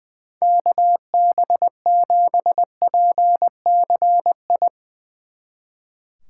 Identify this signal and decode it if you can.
Morse code, heard as a keyed tone, about 22 wpm, 700 Hz. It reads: KB7PCI
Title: Speed Racing - USA Call Signs 20wpm